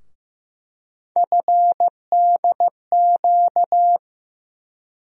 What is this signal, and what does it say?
Morse code, heard as a keyed tone, about 15 wpm, 700 Hz. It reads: FDQ
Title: USA Call Sign Postfixes - 3 Letters 15wpm